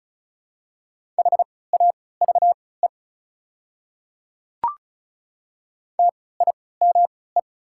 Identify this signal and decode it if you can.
Morse code, heard as a keyed tone, about 36 wpm, spaced as for 20 wpm, 700 Hz. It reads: HAVE TIME
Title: 3x Character Spacing - Top 200 Words - 35wpm